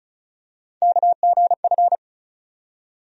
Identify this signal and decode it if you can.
Morse code, heard as a keyed tone, about 36 wpm, 700 Hz. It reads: KGF
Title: USA Call Sign Postfixes - 3 Letters 35wpm